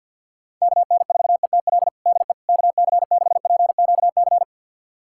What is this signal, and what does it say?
Morse code, heard as a keyed tone, about 50 wpm, 700 Hz. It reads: KN4ETL DE KC6PQC
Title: USA Callsign DE Callsign 50wpm